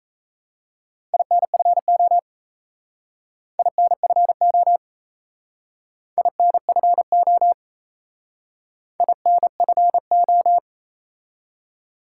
Morse code, heard as a keyed tone, about 34 wpm, 700 Hz. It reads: INFO INFO INFO INFO
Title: Speed-Racing Top 180 QSO Elements 28wpm